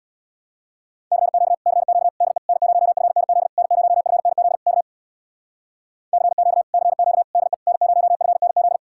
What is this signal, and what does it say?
Morse code, heard as a keyed tone, about 66 wpm, 700 Hz. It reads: CQ CQ DE N0FNZ N0FNZ K CQ CQ DE N0FNZ